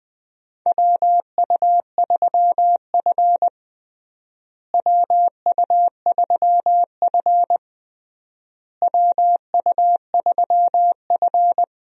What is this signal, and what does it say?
Morse code, heard as a keyed tone, about 20 wpm, 700 Hz. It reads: WU3F WU3F WU3F
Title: USA Call Signs - 4 Characters - Repeated 3x - 20wpm